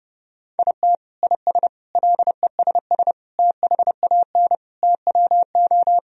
Morse code, heard as a keyed tone, about 30 wpm, 700 Hz. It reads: IT IS LESS THAN TWO
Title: Sentences from Top 1000 Words - Review Part 2 - 30wpm